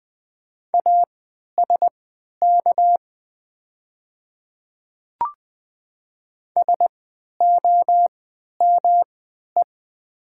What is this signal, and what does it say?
Morse code, heard as a keyed tone, about 20 wpm, spaced as for 11 wpm, 700 Hz. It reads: ASK SOME
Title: 3x Character Spacing - Top 200 Words - 20wpm